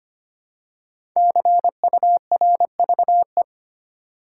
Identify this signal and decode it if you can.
Morse code, heard as a keyed tone, about 25 wpm, 700 Hz. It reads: CURVE
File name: 5 Letter Words - First 3 Letters Given - 25wpm